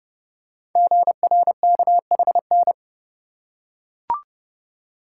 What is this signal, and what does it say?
Morse code, heard as a keyed tone, about 30 wpm, 700 Hz. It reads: GRKHN
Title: Code Groups - 5 Letter:Numbers - 30wpm